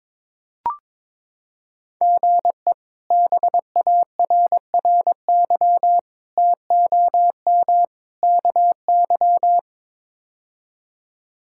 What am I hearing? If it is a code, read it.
Morse code, heard as a keyed tone, about 22 wpm, 700 Hz. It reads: GE BARRY TOM KY
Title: K1USN SST QSO Practice - 22wpm